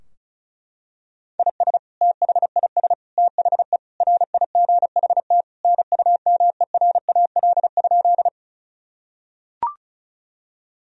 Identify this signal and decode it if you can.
Morse code, heard as a keyed tone, about 36 wpm, 700 Hz. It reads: IS THIS THE RIGHT NUMERAL?
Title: Sentences from Top 1000 Words - Review Part 2 - 35wpm